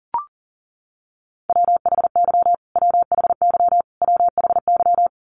Morse code, heard as a keyed tone, about 37 wpm, 700 Hz. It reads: WHY WHY WHY
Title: Top 200 Words Repeated 40wpm